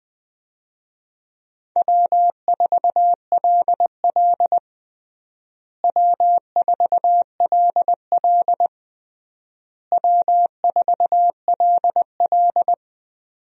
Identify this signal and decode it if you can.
Morse code, heard as a keyed tone, about 21 wpm, 700 Hz. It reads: W4LL W4LL W4LL
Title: USA Call Signs - 4 Characters - Repeated 3x - 20wpm